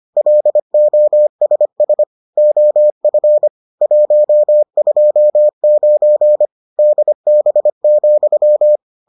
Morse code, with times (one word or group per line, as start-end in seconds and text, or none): 0.2-2.0 LOSS
2.4-3.5 OF
3.8-6.5 129
6.8-8.8 DB,